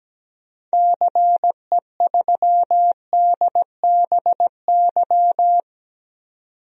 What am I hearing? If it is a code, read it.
Morse code, heard as a keyed tone, about 17 wpm, 700 Hz. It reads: CE3DBY
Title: DXCC-100 Call Signs - 6 Characters - 17wpm